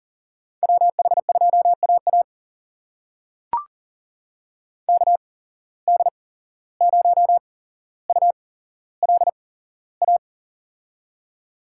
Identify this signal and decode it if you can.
Morse code, heard as a keyed tone, about 38 wpm, spaced as for 16 wpm, 700 Hz. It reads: WH2AA KD0ULA
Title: USA Call Signs - 8x Character Spacing - 40wpm